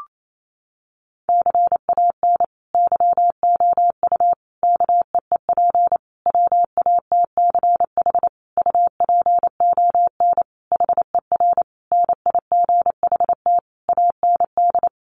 Morse code, heard as a keyed tone, about 28 wpm, 700 Hz. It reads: CAN YOU KEEP WATCH UPON HER NIGHT AND
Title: Sentences from Top 1100 Words 28wpm